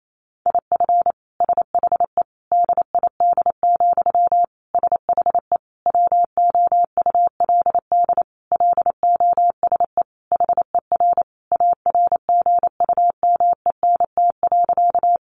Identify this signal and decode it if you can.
Morse code, heard as a keyed tone, about 28 wpm, 700 Hz. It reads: IF SHE DID, SHE WOULD LOSE HER ARGUMENT.
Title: Sentences from Top 1400 Words 28wpm